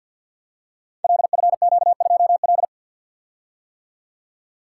Morse code, heard as a keyed tone, about 53 wpm, 700 Hz. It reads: LFQ2L
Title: Code Groups - 5 Letter:Numbers - 50wpm